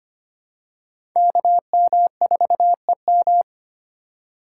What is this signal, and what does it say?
Morse code, heard as a keyed tone, about 25 wpm, 700 Hz. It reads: KM4EM
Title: USA Call Signs - 5 Characters - 6x Character Spacing - 25wpm